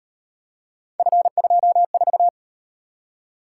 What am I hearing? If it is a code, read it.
Morse code, heard as a keyed tone, about 38 wpm, 700 Hz. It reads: F24